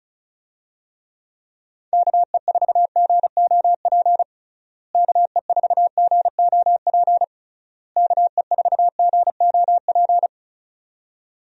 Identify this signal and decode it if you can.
Morse code, heard as a keyed tone, about 36 wpm, 700 Hz. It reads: KE4GOP KE4GOP KE4GOP
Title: USA Call Signs - 6 Characters - Repeated 3x - 35wpm